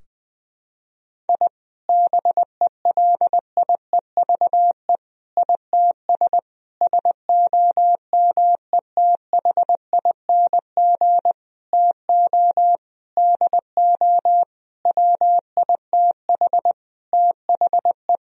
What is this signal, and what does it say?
Morse code, heard as a keyed tone, about 19 wpm, 700 Hz. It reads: I BELIEVE ITS SOMETHING TO DO WITH THE
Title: Sentences from Top 1000 Words - Review Part 3 - 20wpm